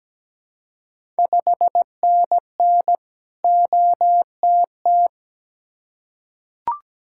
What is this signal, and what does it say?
Morse code, heard as a keyed tone, about 17 wpm, 700 Hz. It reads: 5NN OTT
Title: Ontario QSO Party Exchange 17wpm